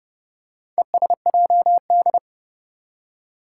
Morse code, heard as a keyed tone, about 30 wpm, 700 Hz. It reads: ESJD
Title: Four Challenging Letter-Number Combinations 30wpm